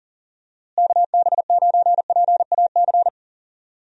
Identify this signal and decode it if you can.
Morse code, heard as a keyed tone, about 40 wpm, 700 Hz. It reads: KB9PAC
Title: USA Call Signs - 6 Characters - Repeated 2x - 40wpm